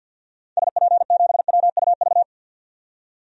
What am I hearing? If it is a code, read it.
Morse code, heard as a keyed tone, about 51 wpm, 700 Hz. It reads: SP7WUV
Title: DXCC-100 Call Signs - 6 Characters - 50wpm